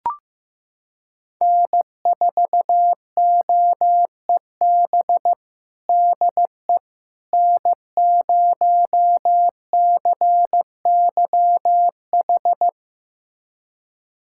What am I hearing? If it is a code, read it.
Morse code, heard as a keyed tone, about 16 wpm, 700 Hz. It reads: N4OEB DE N0CYH
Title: USA Callsign DE Callsign 15wpm